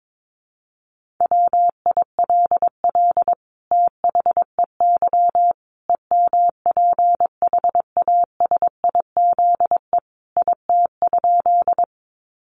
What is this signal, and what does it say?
Morse code, heard as a keyed tone, about 22 wpm, 700 Hz. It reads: WILL THEY EMPHASIZE IT?
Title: Sentences from Top 1500 Words 22wpm